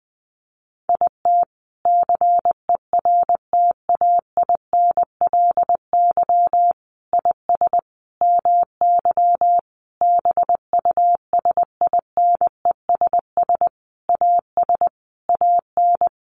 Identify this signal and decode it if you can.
Morse code, heard as a keyed tone, about 20 wpm, 700 Hz. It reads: IT CERTAINLY IS MY BUSINESS AS AN
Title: Sentences from Top 1200 Words 20wpm